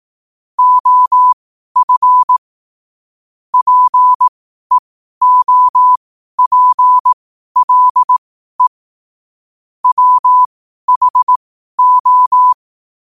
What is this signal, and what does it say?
Morse code, heard as a keyed tone, about 18 wpm, spaced as for 12 wpm, 1000 Hz. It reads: OF PEOPLE WHO
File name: qotdfiles / 12wpm